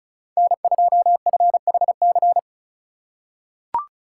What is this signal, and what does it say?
Morse code, heard as a keyed tone, about 34 wpm, 700 Hz. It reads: N2FHC
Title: USA Call Signs - 5 Characters - 4x Character Spacing - 35wpm